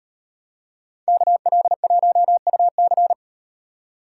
Morse code, heard as a keyed tone, about 35 wpm, 700 Hz. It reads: KL1UC